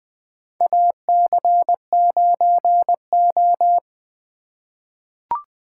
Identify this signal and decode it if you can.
Morse code, heard as a keyed tone, about 20 wpm, 700 Hz. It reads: AC9O
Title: USA Call Signs - 4 Characters - 2x Character Spacing - 20wpm